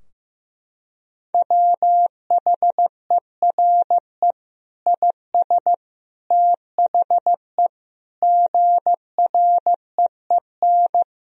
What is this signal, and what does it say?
Morse code, heard as a keyed tone, about 15 wpm, 700 Hz. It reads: WHERE IS THE GREEN
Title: Sentences from Top 1000 Words - Review Part 2 - 15wpm